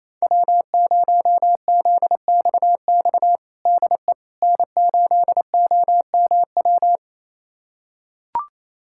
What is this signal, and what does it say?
Morse code, heard as a keyed tone, about 28 wpm, 700 Hz. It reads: W0ZXX DE N8OMW
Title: USA Callsign DE Callsign - 5 Characters - 28wpm